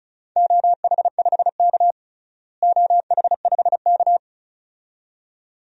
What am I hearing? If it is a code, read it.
Morse code, heard as a keyed tone, about 34 wpm, 700 Hz. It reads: OH5K OH5K
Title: DXCC-100 Call Signs - 4 Characters - Repeated 2x - 35wpm